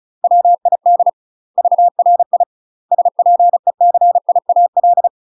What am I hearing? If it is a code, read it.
Morse code, heard as a keyed tone, about 34 wpm, 700 Hz. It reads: WID VRI SPECIAL
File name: UK MASTER QSOs TRAINING / 35WPM / UK MASTER QSOs 0021